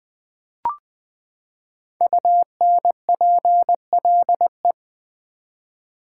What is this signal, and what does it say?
Morse code, heard as a keyed tone, about 20 wpm, 700 Hz. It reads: UNPLE
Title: Build A Word - 5 Letters At A Time - 10-Letter Words - 20wpm